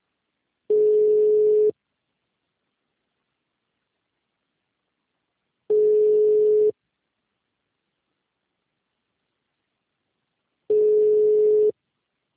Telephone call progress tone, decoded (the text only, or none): ringback tone